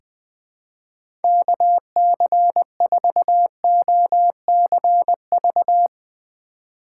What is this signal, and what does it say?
Morse code, heard as a keyed tone, about 20 wpm, 700 Hz. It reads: KC4OCV
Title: USA Call Signs - 6 Characters - 3x Character Spacing - 20wpm